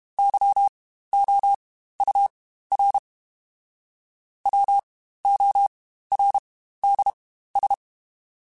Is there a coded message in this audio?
Yes: Morse code from 0.2 to 7.7 s, heard as a keyed tone, about 32 wpm, spaced as for 15 wpm, 800 Hz. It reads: YOUR WORDS